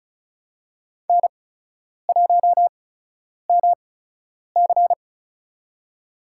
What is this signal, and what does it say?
Morse code, heard as a keyed tone, about 36 wpm, spaced as for 10 wpm, 700 Hz. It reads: N1MC
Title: USA Call Signs - 4 Characters - 8x Character Spacing - 35wpm